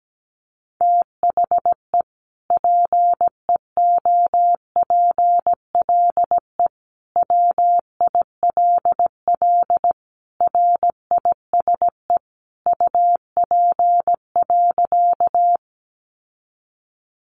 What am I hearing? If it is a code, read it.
Morse code, heard as a keyed tone, about 17 wpm, 700 Hz. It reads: THE PEOPLE WILL RISE UP.
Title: Sentences from Top 900 Words 17wpm